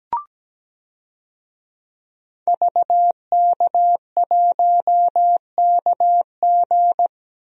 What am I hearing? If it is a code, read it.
Morse code, heard as a keyed tone, about 17 wpm, 700 Hz. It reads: VK1KG
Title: DXCC-100 Call Signs - 5 Characters - Repeated 2x - 17wpm